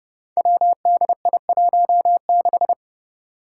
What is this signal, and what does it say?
Morse code, heard as a keyed tone, about 30 wpm, 700 Hz. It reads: WDI16
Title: Five Challenging Letter-Number Combinations 30wpm